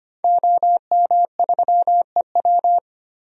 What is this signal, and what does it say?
Morse code, heard as a keyed tone, about 25 wpm, 700 Hz. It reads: OM3EW
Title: DXCC-100 Call Signs - 5 Characters - 25wpm